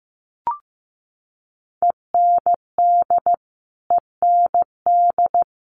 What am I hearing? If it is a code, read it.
Morse code, heard as a keyed tone, about 15 wpm, 700 Hz. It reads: END END
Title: Top 200 Words Repeated 15wpm